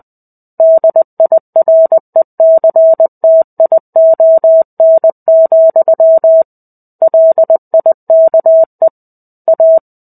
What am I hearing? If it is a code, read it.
Morse code, heard as a keyed tone, about 20 wpm, 650 Hz. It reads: DIRECTION, LIKE A
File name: 05-Chapter-05 20 WPM